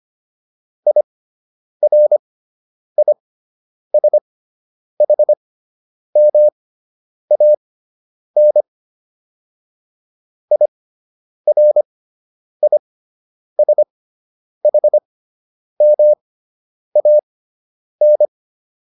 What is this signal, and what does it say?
Morse code, heard as a keyed tone, about 25 wpm, spaced as for 9 wpm, 600 Hz. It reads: IRISHMAN IRISHMAN